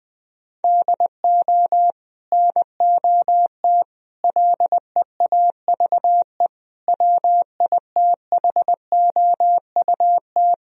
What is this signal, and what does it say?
Morse code, heard as a keyed tone, about 21 wpm, 700 Hz. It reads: DO NOT LEAVE WITHOUT
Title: Sentences from Top 1000 Words - Review Part 3 - 20wpm